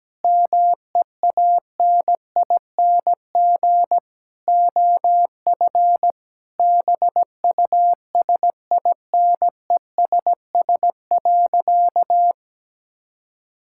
Morse code, heard as a keyed tone, about 17 wpm, 700 Hz. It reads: MEANING OF BUSINESS.
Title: Sentences from Top 1000 Words - Review Part 6 - 17wpm